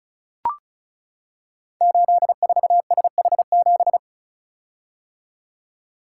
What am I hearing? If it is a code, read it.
Morse code, heard as a keyed tone, about 36 wpm, 700 Hz. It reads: 84SH7
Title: Code Groups - 5 Letter:Numbers - 35wpm